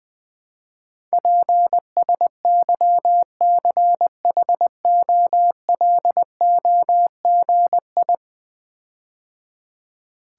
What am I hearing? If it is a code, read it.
Morse code, heard as a keyed tone, about 20 wpm, 700 Hz. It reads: PSYCHOLOGI
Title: Build A Word - 5 Letters At A Time - 12-Letter Words - 20wpm